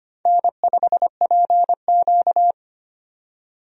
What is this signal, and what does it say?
Morse code, heard as a keyed tone, about 25 wpm, 700 Hz. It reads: N5PQ
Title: USA Call Signs - 4 Characters - 25wpm